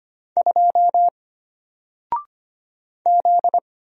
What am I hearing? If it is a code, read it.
Morse code, heard as a keyed tone, about 25 wpm, 700 Hz. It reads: 2 Z
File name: Single-Letter-Number-25wpm